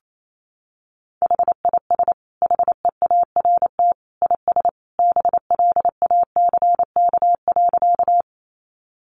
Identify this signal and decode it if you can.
Morse code, heard as a keyed tone, about 28 wpm, 700 Hz. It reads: HIS HEART IS BLACK.
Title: Sentences from Top 1200 Words 28wpm